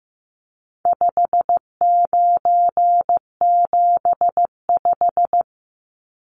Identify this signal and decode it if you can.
Morse code, heard as a keyed tone, about 15 wpm, 700 Hz. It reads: 5975
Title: Numbers - 4-Digit 15wpm